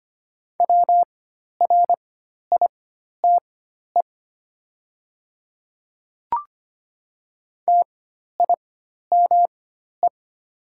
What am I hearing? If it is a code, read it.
Morse code, heard as a keyed tone, about 25 wpm, spaced as for 12 wpm, 700 Hz. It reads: WRITE TIME
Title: 4x Character Spacing - Top 100 Words - 25wpm